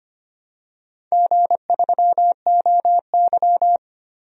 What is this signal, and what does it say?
Morse code, heard as a keyed tone, about 25 wpm, 700 Hz. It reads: G3OY
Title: DXCC-100 Call Signs - 4 Characters - 25wpm